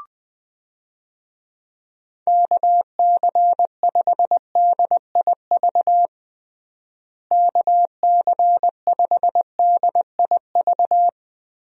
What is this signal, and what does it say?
Morse code, heard as a keyed tone, about 20 wpm, 700 Hz. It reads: KC5DIV KC5DIV
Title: USA Call Signs - 6 Characters - Repeated 2x - 20wpm